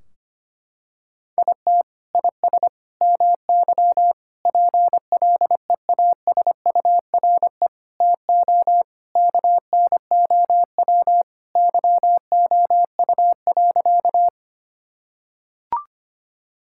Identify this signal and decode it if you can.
Morse code, heard as a keyed tone, about 25 wpm, 700 Hz. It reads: IT IS MY PLEASURE TO KNOW YOU.
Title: Sentences from Top 1600 Words 25wpm